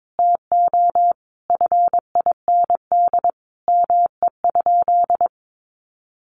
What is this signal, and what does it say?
Morse code, heard as a keyed tone, about 22 wpm, 700 Hz. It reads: TO FIND ME?
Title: Sentences from Top 1400 Words 22wpm